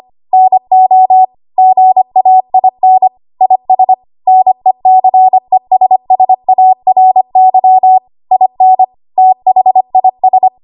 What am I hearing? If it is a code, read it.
Morse code, heard as a keyed tone, about 25 wpm, 750 Hz. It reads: NO GAIN IS NECESSARY IN THIS
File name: Archive / 25 WPM / 25WPM